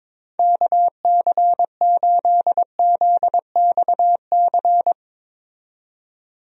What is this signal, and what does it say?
Morse code, heard as a keyed tone, about 22 wpm, 700 Hz. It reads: KC8ZXC
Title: USA Call Signs - 6 Characters - 22wpm